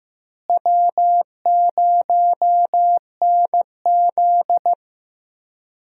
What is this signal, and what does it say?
Morse code, heard as a keyed tone, about 15 wpm, 700 Hz. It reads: W0NZ
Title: USA Call Signs - 4 Characters - Repeated 3x - 15wpm